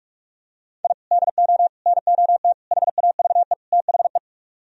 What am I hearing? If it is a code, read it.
Morse code, heard as a keyed tone, about 44 wpm, 700 Hz. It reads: I DO NOT HAVE THE